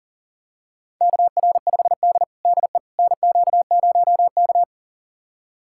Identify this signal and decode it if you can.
Morse code, heard as a keyed tone, about 40 wpm, 700 Hz. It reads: KR5D DE NQ0K
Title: USA Callsign DE Callsign - 4 Characters - 40wpm